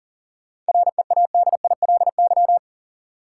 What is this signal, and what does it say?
Morse code, heard as a keyed tone, about 40 wpm, 700 Hz. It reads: READILY
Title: Seven Letter Words - 40wpm